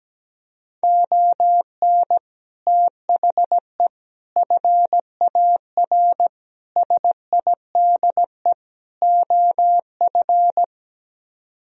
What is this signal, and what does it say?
Morse code, heard as a keyed tone, about 17 wpm, 700 Hz. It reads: ON THE FAR SIDE OF
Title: Speed-Racing - Sets of 5 Words 17wpm